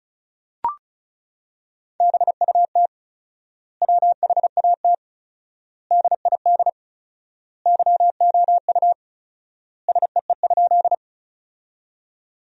Morse code, heard as a keyed tone, about 34 wpm, 700 Hz. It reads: BUT WHAT DID YOU SEE?
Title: 4x Word-Spacing - Sentences from Top 100 Words - 35wpm